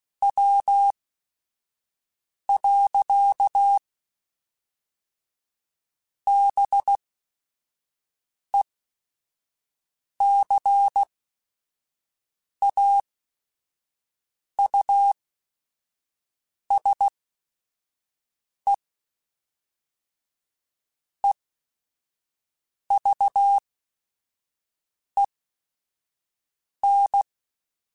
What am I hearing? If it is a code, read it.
Morse code, heard as a keyed tone, about 16 wpm, spaced as for 5 wpm, 800 Hz. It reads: W. BECAUSE EVEN
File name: Technical Subjects / Technical Subjects-mp3s / XL @5